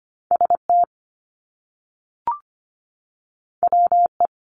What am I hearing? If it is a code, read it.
Morse code, heard as a keyed tone, about 25 wpm, 700 Hz. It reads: ST WE
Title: Top Two Letter Combinations in Words 25wpm